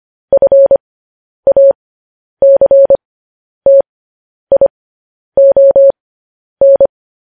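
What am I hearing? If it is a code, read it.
Morse code, heard as a keyed tone, about 25 wpm, spaced as for 10 wpm, 550 Hz. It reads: FACTION
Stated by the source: Morse code transmission of news headlines.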